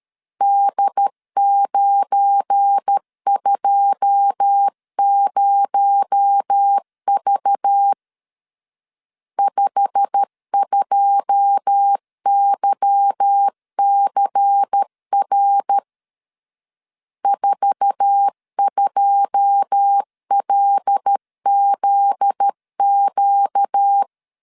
Morse code, heard as a keyed tone, about 13 wpm, 800 Hz. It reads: D920V 52YCR 42LZQ
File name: LEZIONI / 9 2 12wpm 2301